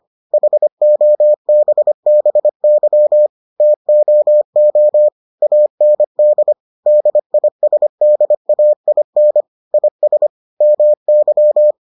Morse code, heard as a keyed tone, about 25 wpm, 600 Hz. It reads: HOBBY TOO AND DISDAIN IS MY